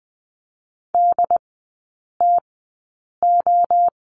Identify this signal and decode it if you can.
Morse code, heard as a keyed tone, about 20 wpm, spaced as for 8 wpm, 700 Hz. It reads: DTO
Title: 067 Lesson 12e - Letter d - Three Characters 20wpm